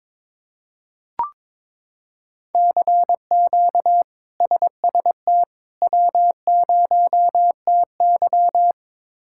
Morse code, heard as a keyed tone, about 22 wpm, 700 Hz. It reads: CQ SST W0TY
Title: K1USN SST QSO Practice - 22wpm